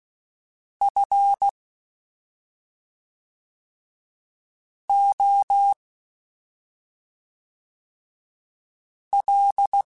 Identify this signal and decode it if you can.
Morse code, heard as a keyed tone, about 16 wpm, 800 Hz. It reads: F O L
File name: Special INGs / Special INGs-mp3s / SlowToFaster ing @5